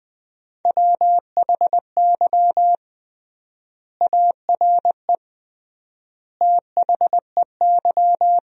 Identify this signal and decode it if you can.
Morse code, heard as a keyed tone, about 20 wpm, 700 Hz. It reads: WHY ARE THEY
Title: 3x Word-Spacing - Sentences from Top 200 Words - 20wpm